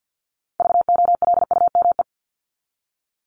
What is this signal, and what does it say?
Morse code, heard as a keyed tone, about 50 wpm, 700 Hz. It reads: VOLUME
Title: Top 1751-2000 Words 50wpm